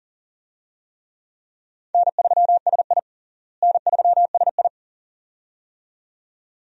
Morse code, heard as a keyed tone, about 40 wpm, 700 Hz. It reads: N3SI N3SI
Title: USA Call Signs - 4 Characters - Repeated 2x - 40wpm